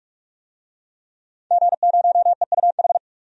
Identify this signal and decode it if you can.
Morse code, heard as a keyed tone, about 46 wpm, 700 Hz. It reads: G0EUH
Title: DXCC-100 Call Signs - 5 Characters - 45wpm